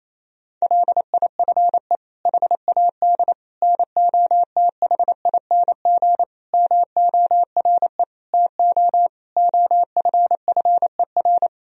Morse code, heard as a keyed tone, about 28 wpm, 700 Hz. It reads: LIFE HAD NOTHING MORE TO OFFER
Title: Sentences from Top 1000 Words - Review Part 4 - 28wpm